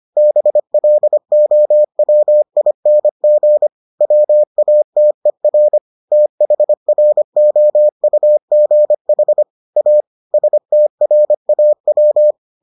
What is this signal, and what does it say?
Morse code, heard as a keyed tone, about 25 wpm, 600 Hz. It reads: BLOWING WATER THROUGH A STRAW